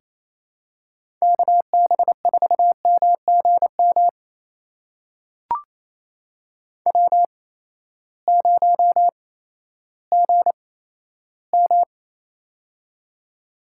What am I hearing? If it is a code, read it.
Morse code, heard as a keyed tone, about 28 wpm, spaced as for 14 wpm, 700 Hz. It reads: KB4MGM W0GM
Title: USA Call Signs - 8x Character Spacing - 28wpm